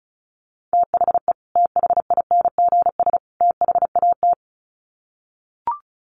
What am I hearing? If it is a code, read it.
Morse code, heard as a keyed tone, about 34 wpm, 700 Hz. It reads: THE THINGS THAT
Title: Sets of 3 Words 35wpm